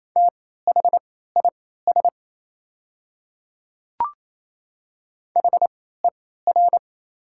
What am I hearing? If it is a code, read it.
Morse code, heard as a keyed tone, about 27 wpm, spaced as for 16 wpm, 700 Hz. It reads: THIS HER
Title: 3x Character Spacing - Top 100 Words - 28wpm